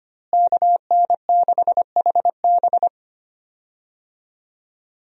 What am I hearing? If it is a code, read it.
Morse code, heard as a keyed tone, about 25 wpm, 700 Hz. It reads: KN6HB
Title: USA Call Signs - 5 Characters - Repeated 3x - 25wpm